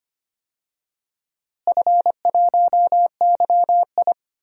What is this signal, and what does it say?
Morse code, heard as a keyed tone, about 25 wpm, 700 Hz. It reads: F1YI
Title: DXCC-100 Call Signs - 4 Characters - Repeated 3x - 25wpm